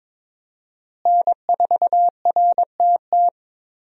Morse code, heard as a keyed tone, about 22 wpm, 700 Hz. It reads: N4RTT
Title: USA Call Signs - 5 Characters - Repeated 2x - 22wpm